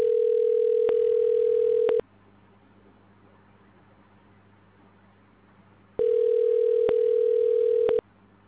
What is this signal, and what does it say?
Ringback tone (North America), 440 + 480 Hz.